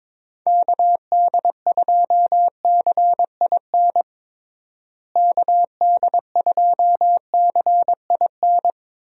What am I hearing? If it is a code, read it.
Morse code, heard as a keyed tone, about 22 wpm, 700 Hz. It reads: KD2CIN KD2CIN
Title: USA Call Signs - 6 Characters - Repeated 3x - 22wpm